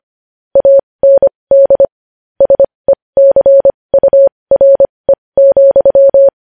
Morse code, heard as a keyed tone, about 25 wpm, spaced as for 20 wpm, 550 Hz. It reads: AND SECURE,
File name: podcast / News headlines-1x-25x20